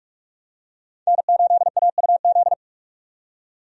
Morse code, heard as a keyed tone, about 44 wpm, 700 Hz. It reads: N8AUZ
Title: USA Call Signs - 5 Characters - 45wpm